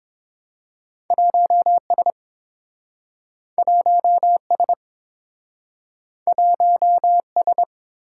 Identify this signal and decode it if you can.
Morse code, heard as a keyed tone, about 30 wpm, 700 Hz. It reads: 1S 1S 1S